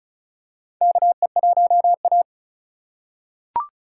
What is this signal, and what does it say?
Morse code, heard as a keyed tone, about 36 wpm, 700 Hz. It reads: KE1A
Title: USA Call Signs - 4 Characters - 2x Character Spacing - 35wpm